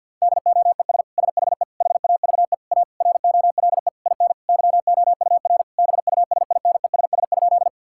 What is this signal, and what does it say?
Morse code, heard as a keyed tone, about 50 wpm, 700 Hz. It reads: DOES SHE HAVE A ROLE IN YOUR BUSINESS?
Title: Sentences from Top 1000 Words - Review Part 6 - 50wpm